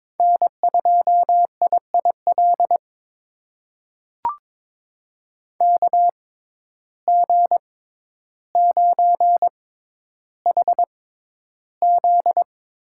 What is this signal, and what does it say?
Morse code, heard as a keyed tone, about 22 wpm, spaced as for 11 wpm, 700 Hz. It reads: N2IIL KG9HZ
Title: USA Call Signs - 5 Characters - 6x Character Spacing - 22wpm